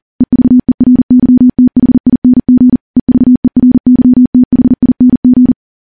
Morse code, heard as a keyed tone, about 40 wpm, 250 Hz. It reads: EVERYTHING EVERYTHING
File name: morse / morse-training-40WPM